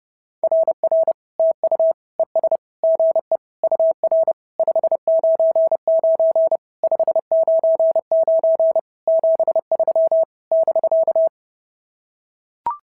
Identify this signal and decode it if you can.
Morse code, heard as a keyed tone, about 30 wpm, 690 Hz. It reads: RR TU ES GE UR 599 599 73 <BK>
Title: SOTA QSO Practice 30wpm